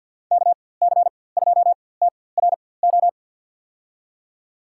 Morse code, heard as a keyed tone, about 52 wpm, spaced as for 23 wpm, 700 Hz. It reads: KC2TRO
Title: USA Call Signs - 6 Characters - 4x Character Spacing - 50wpm